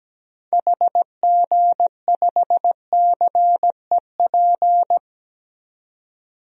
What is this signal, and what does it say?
Morse code, heard as a keyed tone, about 17 wpm, 700 Hz. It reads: HG5CEP
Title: DXCC-100 Call Signs - 6 Characters - 17wpm